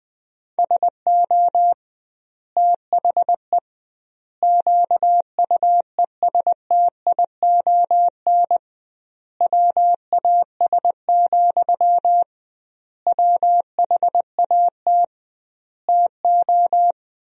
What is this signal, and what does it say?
Morse code, heard as a keyed tone, about 20 wpm, 700 Hz. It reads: SO THE QUESTION WAS, WHAT TO
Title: 2x Word-Spacing - Sentences from Top 400 Words - 20wpm